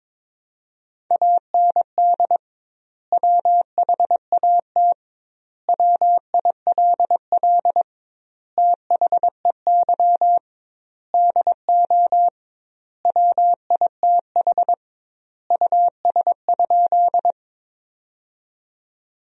Morse code, heard as a keyed tone, about 22 wpm, 700 Hz. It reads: AND WHAT WILL THEY DO WITH US?
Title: 2x Word-Spacing - Sentences from Top 200 Words - 22wpm